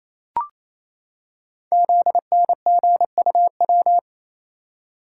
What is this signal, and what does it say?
Morse code, heard as a keyed tone, about 27 wpm, 700 Hz. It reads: ZNGUW
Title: Code Groups - 5 Letters - 28wpm